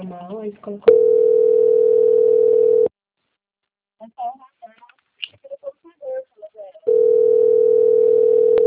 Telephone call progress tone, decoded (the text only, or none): ringback tone